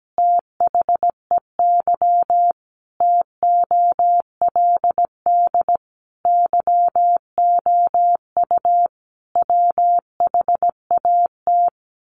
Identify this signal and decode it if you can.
Morse code, heard as a keyed tone, about 17 wpm, 700 Hz. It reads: THEY TOLD YOU WHAT
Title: Sentences from Top 1000 Words 17wpm